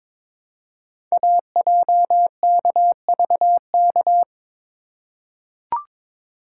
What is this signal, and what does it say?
Morse code, heard as a keyed tone, about 23 wpm, 700 Hz. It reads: AJKVK
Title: Code Groups - 5 Letters - 22wpm